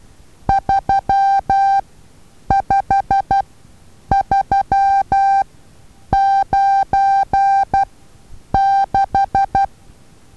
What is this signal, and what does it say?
Morse code, heard as a keyed tone, about 12 wpm, spaced as for 8 wpm, 790 Hz. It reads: 35396
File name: Taviro HA6VA A